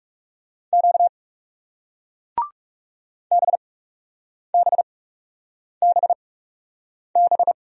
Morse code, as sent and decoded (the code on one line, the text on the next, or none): --.- / -... -... -... -...
Q BBBB